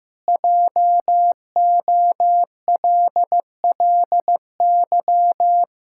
Morse code, heard as a keyed tone, about 15 wpm, 700 Hz. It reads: JOLLY
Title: Pangram Challenge v2 - 15wpm